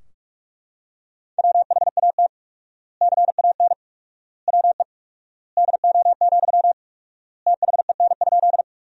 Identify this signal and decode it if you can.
Morse code, heard as a keyed tone, about 42 wpm, 700 Hz. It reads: WHAT CAN WE DO, THEN?
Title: 4x Word-Spacing - Sentences from Top 100 Words - 45wpm